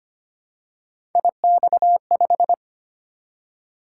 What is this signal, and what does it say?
Morse code, heard as a keyed tone, about 25 wpm, 700 Hz. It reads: IX5
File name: Three Letter-Number Combinations - 25wpm